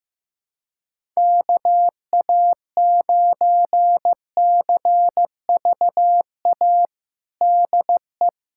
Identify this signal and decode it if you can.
Morse code, heard as a keyed tone, about 15 wpm, 700 Hz. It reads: KA9CVA DE
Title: USA Callsign DE Callsign 15wpm